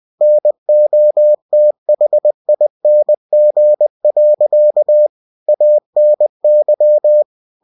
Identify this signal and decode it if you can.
Morse code, heard as a keyed tone, about 20 wpm, 600 Hz. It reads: NOTHING. ANY
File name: cw-books / winnie-the-pooh / winnie-20wpm-page-0107